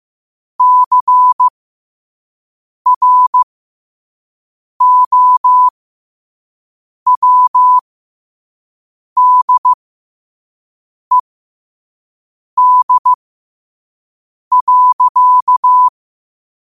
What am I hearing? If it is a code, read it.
Morse code, heard as a keyed tone, about 15 wpm, spaced as for 5 wpm, 1000 Hz. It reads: CROWDED.
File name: qotdfiles / 5wpm